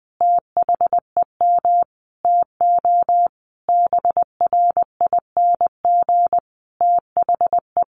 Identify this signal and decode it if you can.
Morse code, heard as a keyed tone, about 20 wpm, 700 Hz. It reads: THEM TO BRING THE
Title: Sentences from Top 500 Words 20wpm